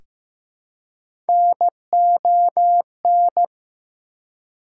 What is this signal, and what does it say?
Morse code, heard as a keyed tone, about 15 wpm, 700 Hz. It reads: NON
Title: Prefixes and Suffixes 15wpm